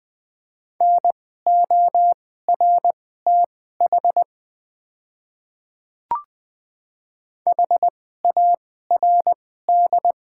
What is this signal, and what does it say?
Morse code, heard as a keyed tone, about 21 wpm, spaced as for 15 wpm, 700 Hz. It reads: NORTH HARD
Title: 2x Character Spacing - Top 300 Words - 20wpm